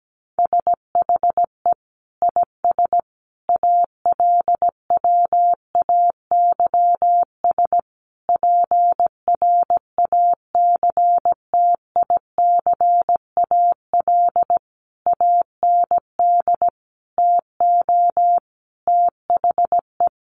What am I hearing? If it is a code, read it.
Morse code, heard as a keyed tone, about 17 wpm, 700 Hz. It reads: SHE IS ALWAYS PRACTICAL AND TO THE
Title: Sentences from Top 1400 Words 17wpm